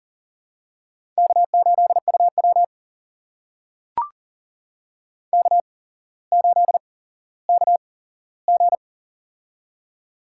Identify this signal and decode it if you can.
Morse code, heard as a keyed tone, about 40 wpm, spaced as for 17 wpm, 700 Hz. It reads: K8UW K8KG
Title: USA Call Signs - 4 Characters - 8x Character Spacing - 40wpm